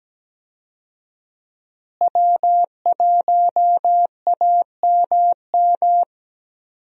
Morse code, heard as a keyed tone, about 17 wpm, 700 Hz. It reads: W1AMM
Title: USA Call Signs - 5 Characters - Repeated 3x - 17wpm